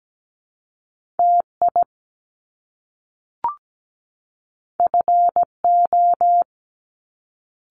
Morse code, heard as a keyed tone, about 17 wpm, 700 Hz. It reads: TI FO
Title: Top Two Letter Combinations in Words 17wpm